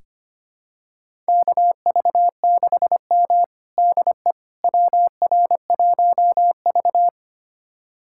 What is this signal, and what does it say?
Morse code, heard as a keyed tone, about 25 wpm, 700 Hz. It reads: KV6M DE WR1V
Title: USA Callsign DE Callsign - 4 Characters - 25wpm